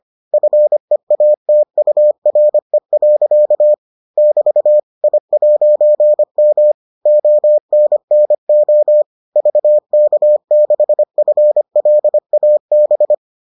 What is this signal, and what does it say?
Morse code, heard as a keyed tone, about 25 wpm, 600 Hz. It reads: FEATURE. <BT> I'M ONNO VK6FLAB